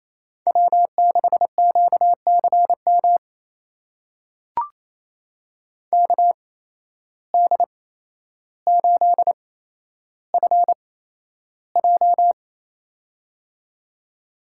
Morse code, heard as a keyed tone, about 27 wpm, spaced as for 12 wpm, 700 Hz. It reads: W6QCM KD8FJ
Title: USA Call Signs - 5 Characters - 8x Character Spacing - 28wpm